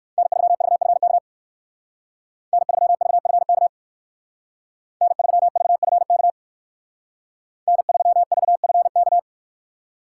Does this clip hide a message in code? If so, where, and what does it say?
Yes: Morse code from 0.2 to 9.2 s, heard as a keyed tone, about 58 wpm, 700 Hz. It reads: N3VFK N3VFK N3VFK N3VFK